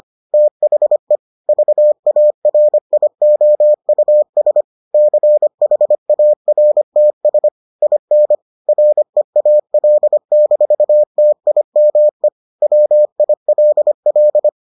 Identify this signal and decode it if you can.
Morse code, heard as a keyed tone, about 25 wpm, 600 Hz. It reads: THE VARIOUS CHARTS IN REAL-TIME WILL